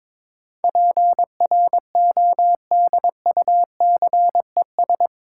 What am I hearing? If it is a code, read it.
Morse code, heard as a keyed tone, about 22 wpm, 700 Hz. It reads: PRODUCES
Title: Sentences from Top 1000 Words - Review Part 6 - New Word Drill - 22wpm